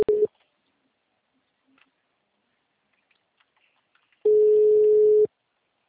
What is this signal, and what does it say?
Ringback tone (Europe, 425 Hz).